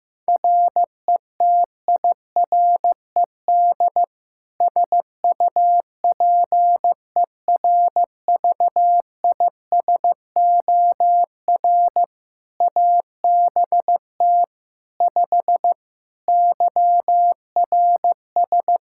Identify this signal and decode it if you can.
Morse code, heard as a keyed tone, about 16 wpm, 700 Hz. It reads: RETIRED SUPERVISOR ABT 5 YRS
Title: Practice QSO - Exchange 3 - More Info - 15wpm